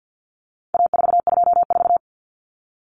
Morse code, heard as a keyed tone, about 48 wpm, 700 Hz. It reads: A424
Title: Common Cut Numbers - 4-Digit 50wpm